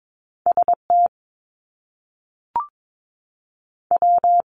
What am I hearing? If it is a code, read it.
Morse code, heard as a keyed tone, about 22 wpm, 700 Hz. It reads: ST W
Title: Top Two Letter Combinations in Words 22wpm